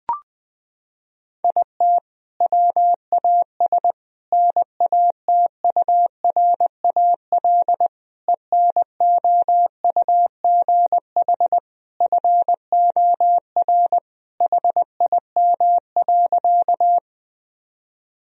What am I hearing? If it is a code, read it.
Morse code, heard as a keyed tone, about 20 wpm, 700 Hz. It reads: IT WAS NATURAL ENOUGH FOR HIM.
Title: Sentences from Top 1000 Words - Review Part 4 - 20wpm